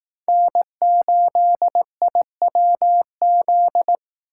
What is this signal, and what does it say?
Morse code, heard as a keyed tone, about 18 wpm, 700 Hz. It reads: N8IWZ
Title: Speed Racing - USA 1x3 Call Signs 15wpm